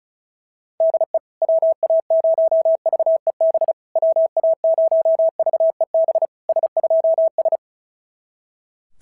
Morse code, heard as a keyed tone, about 36 wpm, 690 Hz. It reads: DE WA0VEB WA0VEB S2S
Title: SOTA QSO Practice 35wpm